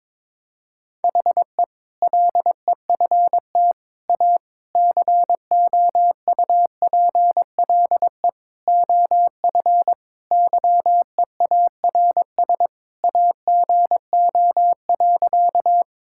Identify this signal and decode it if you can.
Morse code, heard as a keyed tone, about 22 wpm, 700 Hz. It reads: HE LEFT A COUPLE OF YEARS AGO.
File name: practice / 22wpm / Sentences with 4-Word Phrases - 22wpm